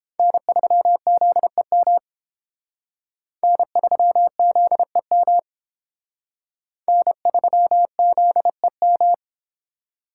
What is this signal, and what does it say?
Morse code, heard as a keyed tone, about 28 wpm, 700 Hz. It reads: N3ZEM N3ZEM N3ZEM